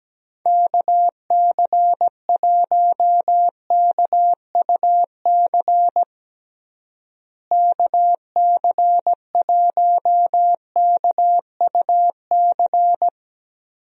Morse code, heard as a keyed tone, about 17 wpm, 700 Hz. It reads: KC1KUC KC1KUC
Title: USA Call Signs - 6 Characters - Repeated 3x - 17wpm